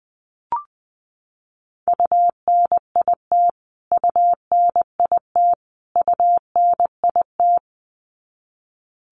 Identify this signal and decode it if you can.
Morse code, heard as a keyed tone, about 20 wpm, 700 Hz. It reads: UNIT UNIT UNIT
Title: Top 500 Words Repeated 20wpm